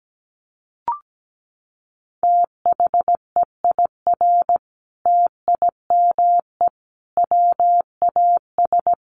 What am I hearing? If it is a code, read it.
Morse code, heard as a keyed tone, about 17 wpm, 700 Hz. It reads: THEIR TIME WAS
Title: Sentences from Top 1300 Words 17wpm